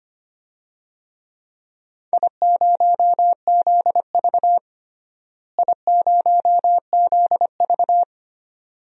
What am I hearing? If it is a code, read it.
Morse code, heard as a keyed tone, about 25 wpm, 700 Hz. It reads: I0ZV I0ZV
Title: DXCC-100 Call Signs - 4 Characters - Repeated 2x - 25wpm